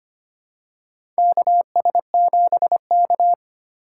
Morse code, heard as a keyed tone, about 25 wpm, 700 Hz. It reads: KS7K
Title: USA Call Signs - 4 Characters - 2x Character Spacing - 25wpm